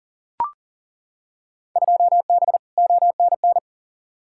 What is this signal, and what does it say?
Morse code, heard as a keyed tone, about 40 wpm, 700 Hz. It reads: 2B ONN
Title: Field Day Exchanges - 40wpm